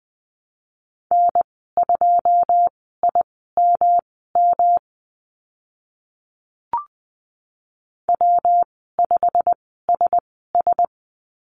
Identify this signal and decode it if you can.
Morse code, heard as a keyed tone, about 20 wpm, spaced as for 14 wpm, 700 Hz. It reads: N2IMM W5SS
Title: 128 Lesson 21g - Number 6 - Call Signs 20wpm